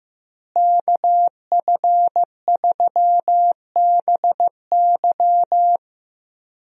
Morse code, heard as a keyed tone, about 15 wpm, 700 Hz. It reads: KF3BY
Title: USA Call Signs - 5 Characters - Repeated 3x - 15wpm